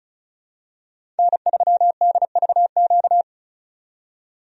Morse code, heard as a keyed tone, about 34 wpm, 700 Hz. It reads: N3DVQ
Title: USA Call Signs - 5 Characters - 35wpm